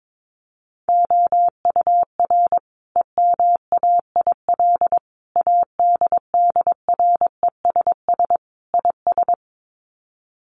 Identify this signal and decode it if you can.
Morse code, heard as a keyed tone, about 22 wpm, 700 Hz. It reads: OUR EMAIL ADDRESS IS
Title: Sets of 5 Words Encore2 22wpm